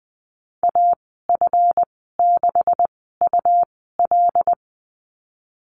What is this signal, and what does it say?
Morse code, heard as a keyed tone, about 20 wpm, spaced as for 14 wpm, 700 Hz. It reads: AF6UL
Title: 179 Lesson 29g - Slash - Call Signs 20wpm